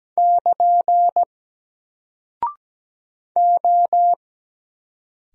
Morse code, heard as a keyed tone, about 17 wpm, 700 Hz. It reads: <KN> O
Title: Single Character (Letter:Number:Punctuation:Prosign) 17wpm